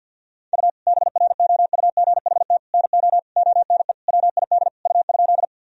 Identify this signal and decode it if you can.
Morse code, heard as a keyed tone, about 50 wpm, 700 Hz. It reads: U BROUGHT NO ONE WID U?
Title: Sentences from Top 500 Words and Abbreviations 50wpm